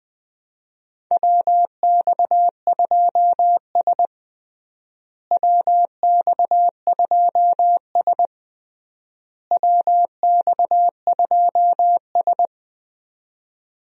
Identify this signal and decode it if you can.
Morse code, heard as a keyed tone, about 20 wpm, 700 Hz. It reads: WX2S WX2S WX2S
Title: USA Call Signs - 4 Characters - Repeated 3x - 20wpm